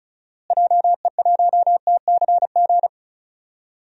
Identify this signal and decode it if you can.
Morse code, heard as a keyed tone, about 36 wpm, 700 Hz. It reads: JE1TCG
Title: DXCC-100 Call Signs - 6 Characters - Repeated 3x - 35wpm